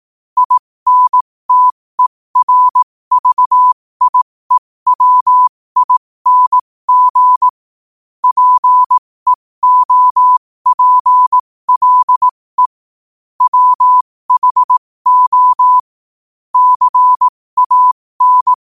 Morse code, heard as a keyed tone, about 18 wpm, spaced as for 15 wpm, 1000 Hz. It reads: INTERVIEWING PEOPLE WHO CAN
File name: qotdfiles / 15wpm